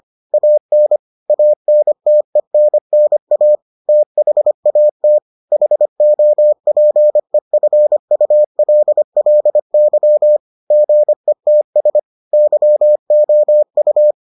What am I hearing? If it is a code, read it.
Morse code, heard as a keyed tone, about 25 wpm, 600 Hz. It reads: AN ANTENNA THAT HOPEFULLY GETS YOU